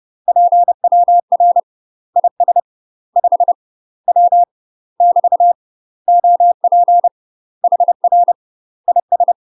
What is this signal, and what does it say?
Morse code, heard as a keyed tone, about 30 wpm, 700 Hz. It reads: PWR IS 5 W <BT> OP HR IS
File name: UK MASTER QSOs TRAINING / 30WPM / UK MASTER QSOs 0036